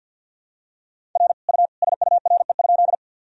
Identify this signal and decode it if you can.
Morse code, heard as a keyed tone, about 51 wpm, 700 Hz. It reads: R U SURE?
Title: Sentences from Top 300 Words and Abbreviations 50wpm